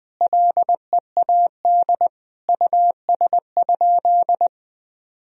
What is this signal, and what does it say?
Morse code, heard as a keyed tone, about 20 wpm, 700 Hz. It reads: LEAD US?
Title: Sentences from Top 1000 Words - Review Part 2 - 20wpm